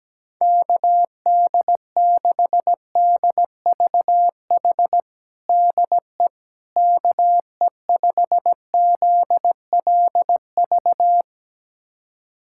Morse code, heard as a keyed tone, about 17 wpm, 700 Hz. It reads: KD6DVH DE KE5ZLV
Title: USA Callsign DE Callsign 17wpm